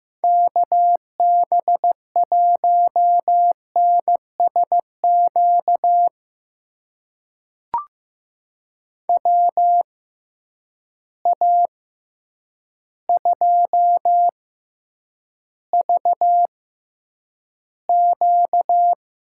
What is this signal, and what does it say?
Morse code, heard as a keyed tone, about 15 wpm, spaced as for 8 wpm, 700 Hz. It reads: KB1NSQ WA2VQ
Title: USA Call Signs - 6 Characters - 6x Character Spacing - 15wpm